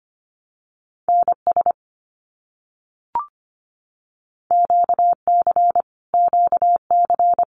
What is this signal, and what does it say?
Morse code, heard as a keyed tone, about 25 wpm, 700 Hz. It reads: NS QC QC
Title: Canadian Provinces and Territories - Repeated 3x - 25wpm